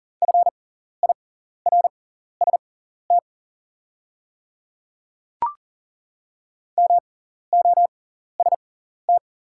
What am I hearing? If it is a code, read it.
Morse code, heard as a keyed tone, about 40 wpm, spaced as for 14 wpm, 690 Hz. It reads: FIRST MOST
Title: 6x Character Spacing - Top 200 Words - 40wpm